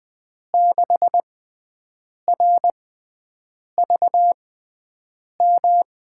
Morse code, heard as a keyed tone, about 20 wpm, spaced as for 7 wpm, 700 Hz. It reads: 6RVM
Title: USA Call Signs - 6x Character Spacing - 20wpm